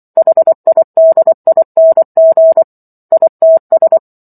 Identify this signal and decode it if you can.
Morse code, heard as a keyed tone, about 24 wpm, 650 Hz. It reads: HIDING ITS
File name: Chapter 15 - 24 WPM